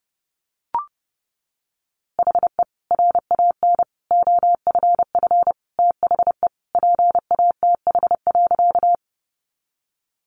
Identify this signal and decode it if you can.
Morse code, heard as a keyed tone, about 30 wpm, 700 Hz. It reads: HE RAN OFF THE PATH.
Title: A Sound Of Thunder 30wpm